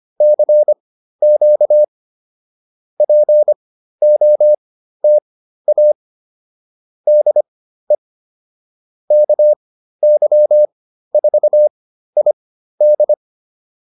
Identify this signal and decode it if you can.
Morse code, heard as a keyed tone, about 25 wpm, spaced as for 13 wpm, 600 Hz. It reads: CQ POTA DE KY4ID